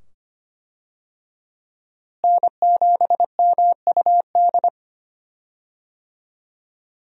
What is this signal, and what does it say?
Morse code, heard as a keyed tone, about 25 wpm, 690 Hz. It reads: N7MUD
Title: USA Call Signs - 5 Characters - Repeated 2x - 25wpm